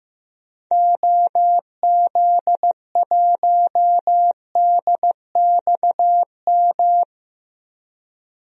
Morse code, heard as a keyed tone, about 16 wpm, 700 Hz. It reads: OZ1DXM
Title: DXCC-100 Call Signs - 6 Characters - 15wpm